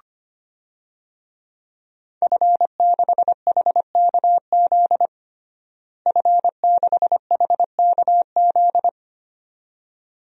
Morse code, heard as a keyed tone, about 25 wpm, 700 Hz. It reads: F6HKZ F6HKZ
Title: DXCC-100 Call Signs - 5 Characters - Repeated 2x - 25wpm